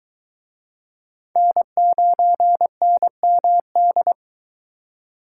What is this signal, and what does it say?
Morse code, heard as a keyed tone, about 22 wpm, 700 Hz. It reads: N9NMD